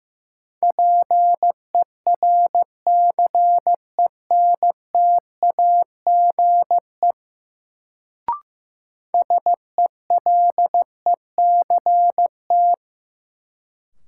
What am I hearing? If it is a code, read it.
Morse code, heard as a keyed tone, about 15 wpm, 700 Hz. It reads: PERCENTAGE SELECT
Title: Sentences from Top 1300 Words - New Word Drill - 15wpm